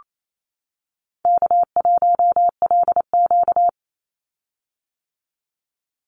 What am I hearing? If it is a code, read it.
Morse code, heard as a keyed tone, about 28 wpm, 700 Hz. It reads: K1LQ
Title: USA Call Signs 28wpm